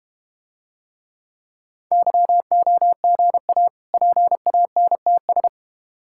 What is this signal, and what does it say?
Morse code, heard as a keyed tone, about 33 wpm, 700 Hz. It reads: YOGA PANTS